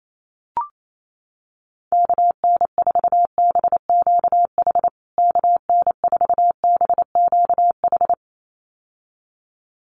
Morse code, heard as a keyed tone, about 28 wpm, 700 Hz. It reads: KN4BQH KN4BQH
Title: USA Call Signs Repeated 2x 28wpm